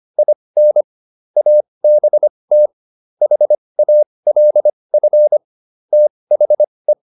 Morse code, heard as a keyed tone, about 25 wpm, spaced as for 20 wpm, 600 Hz. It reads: IN ABT HALF THE